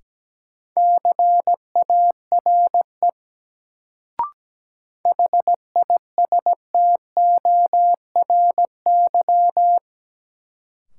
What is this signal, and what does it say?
Morse code, heard as a keyed tone, about 17 wpm, 700 Hz. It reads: CARE HISTORY
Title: Sets of 3 Words - New Word Drill - 17wpm